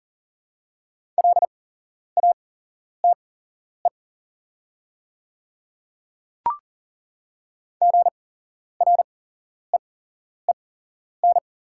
Morse code, heard as a keyed tone, about 40 wpm, spaced as for 11 wpm, 690 Hz. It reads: LATE GREEN
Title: 8x Character Spacing - Top 500 Words - 40wpm